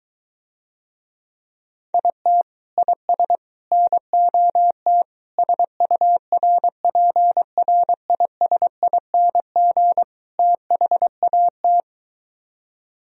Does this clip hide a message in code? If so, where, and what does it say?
Yes: Morse code from 1.9 to 11.8 s, heard as a keyed tone, about 23 wpm, 700 Hz. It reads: IT IS NOT SURPRISING THAT